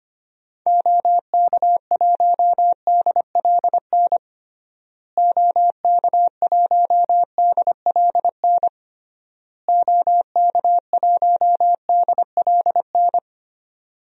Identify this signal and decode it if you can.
Morse code, heard as a keyed tone, about 25 wpm, 700 Hz. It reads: OK1DLN OK1DLN OK1DLN
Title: DXCC-100 Call Signs - 6 Characters - Repeated 3x - 25wpm